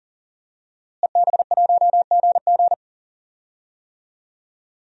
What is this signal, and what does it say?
Morse code, heard as a keyed tone, about 40 wpm, 700 Hz. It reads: EB1GG
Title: DXCC-100 Call Signs - 5 Characters - Repeated 2x - 40wpm